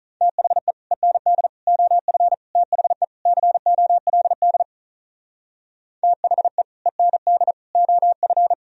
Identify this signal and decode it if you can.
Morse code, heard as a keyed tone, about 42 wpm, 700 Hz. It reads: THE END OF THE COLD THE END OF
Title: Speed-Racing - Sets of 5 Words 35wpm